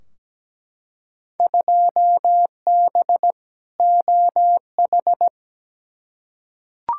Morse code, heard as a keyed tone, about 17 wpm, 700 Hz. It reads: 2B OH
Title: Field Day Exchanges Encore - 17wpm